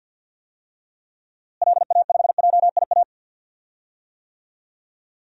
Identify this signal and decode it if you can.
Morse code, heard as a keyed tone, about 46 wpm, 700 Hz. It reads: LA5JIA
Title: DXCC-100 Call Signs - 6 Characters - Repeated 2x - 50wpm